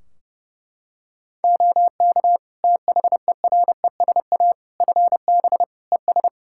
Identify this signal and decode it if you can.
Morse code, heard as a keyed tone, about 32 wpm, 700 Hz. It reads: OK THERESA FB ES
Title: Practice QSO - Exchange 2 - Info - 30wpm